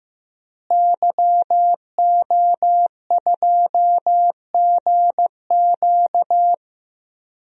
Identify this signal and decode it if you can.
Morse code, heard as a keyed tone, about 15 wpm, 700 Hz. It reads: YO2GQ
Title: DXCC-100 Call Signs - 5 Characters - Repeated 2x - 15wpm